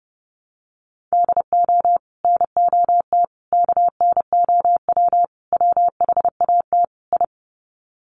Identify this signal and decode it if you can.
Morse code, heard as a keyed tone, about 30 wpm, 700 Hz. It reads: DO NOT KNOW WHAT I
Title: Sets of 5 Words 30wpm